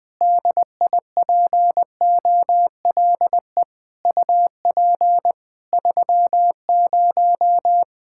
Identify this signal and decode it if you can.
Morse code, heard as a keyed tone, about 20 wpm, 700 Hz. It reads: DIPOLE UP 30
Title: Practice QSO - Exchange 2 - Info - 20wpm